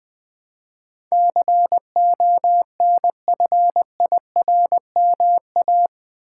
Morse code, heard as a keyed tone, about 20 wpm, 700 Hz. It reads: CONFIRMA
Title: Build A Word - 4 Letters At A Time - 12-Letter Words - 20wpm